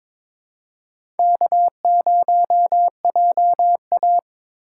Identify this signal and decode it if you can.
Morse code, heard as a keyed tone, about 22 wpm, 700 Hz. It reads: K0JA
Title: USA Call Signs - 4x Character Spacing - 22wpm